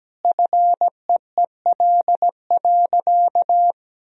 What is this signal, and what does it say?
Morse code, heard as a keyed tone, about 17 wpm, 700 Hz. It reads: FEEL.
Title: Sentences from Top 1000 Words - Review Part 3 - 17wpm